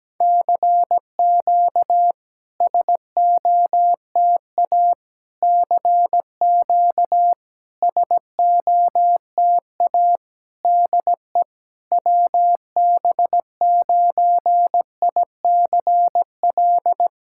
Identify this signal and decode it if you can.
Morse code, heard as a keyed tone, about 17 wpm, 700 Hz. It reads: CQ SOTA CQ SOTA DE WB9ICL
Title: SOTA QSO Practice 17wpm